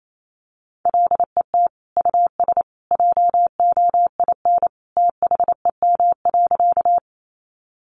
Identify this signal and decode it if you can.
Morse code, heard as a keyed tone, about 28 wpm, 700 Hz. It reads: LET US JOIN THEM.
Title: Sentences from Top 1400 Words 28wpm